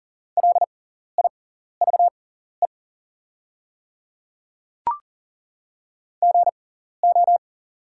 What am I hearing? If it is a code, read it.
Morse code, heard as a keyed tone, about 40 wpm, spaced as for 14 wpm, 700 Hz. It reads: LIVE GO
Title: 6x Character Spacing - Top 200 Words - 40wpm